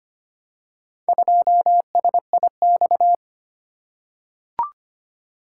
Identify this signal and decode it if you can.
Morse code, heard as a keyed tone, about 25 wpm, 700 Hz. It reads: 2SIX
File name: practice / 25wpm / Four Letter-Number Combinations - 25wpm